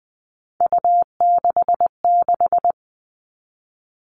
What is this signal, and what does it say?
Morse code, heard as a keyed tone, about 20 wpm, 700 Hz. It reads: U66
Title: All Cut Numbers - 3-Digit 20wpm